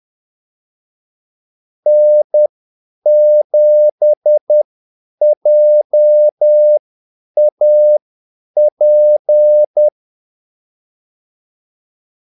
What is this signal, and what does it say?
Morse code, heard as a keyed tone, about 10 wpm, spaced as for 8 wpm, 600 Hz. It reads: N7JAP
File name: Callsigns-10-8wpm